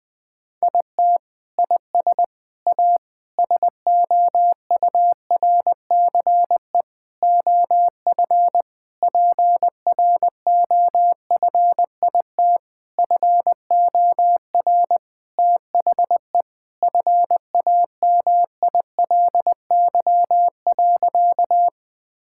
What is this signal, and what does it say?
Morse code, heard as a keyed tone, about 21 wpm, 700 Hz. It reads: IT IS A SOURCE OF PROFIT FOR THE FAMILY.
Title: Sentences from Top 1000 Words - Review Part 5 - 20wpm